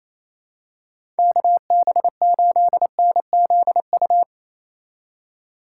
Morse code, heard as a keyed tone, about 28 wpm, 700 Hz. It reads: KB8NZU
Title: USA Call Signs - 6 Characters - 3x Character Spacing - 28wpm